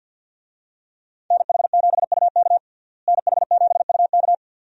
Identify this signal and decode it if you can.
Morse code, heard as a keyed tone, about 50 wpm, 700 Hz. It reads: NH7UK NH7UK
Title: USA Call Signs - 5 Characters - Repeated 2x - 50wpm